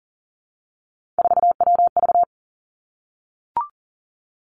Morse code, heard as a keyed tone, about 41 wpm, 700 Hz. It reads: 4WV
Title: Three Challenging Letter-Number Combinations 40wpm